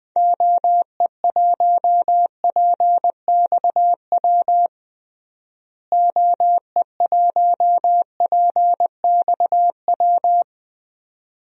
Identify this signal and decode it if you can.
Morse code, heard as a keyed tone, about 20 wpm, 700 Hz. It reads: OE1PXW OE1PXW
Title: DXCC-100 Call Signs - 6 Characters - Repeated 2x - 20wpm